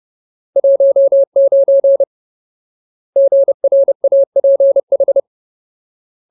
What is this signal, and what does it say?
Morse code, heard as a keyed tone, about 30 wpm, 550 Hz. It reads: 19 GRAPH
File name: Ultimate Word List / 30WPM Ultimate